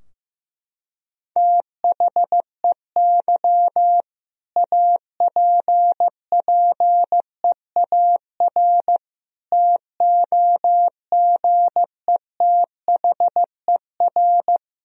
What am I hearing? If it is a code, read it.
Morse code, heard as a keyed tone, about 15 wpm, 700 Hz. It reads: THEY APPEAR TOGETHER
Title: Sentences from Top 1000 Words - Review Part 2 - 15wpm